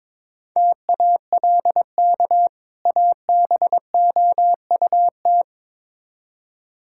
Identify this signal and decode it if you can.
Morse code, heard as a keyed tone, about 22 wpm, 700 Hz. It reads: TALK ABOUT
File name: Build A Phrase - 2-Word Phrases - 22wpm